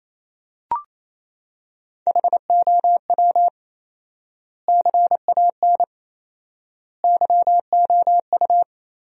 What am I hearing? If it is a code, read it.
Morse code, heard as a keyed tone, about 28 wpm, 700 Hz. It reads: HOW CAN YOU
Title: 4x Word-Spacing - Sentences from Top 100 Words - 28wpm